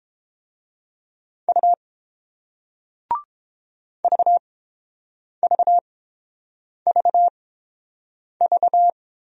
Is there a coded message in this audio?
Yes: Morse code from 1.5 to 8.9 s, heard as a keyed tone, about 30 wpm, spaced as for 7 wpm, 700 Hz. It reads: U VVVV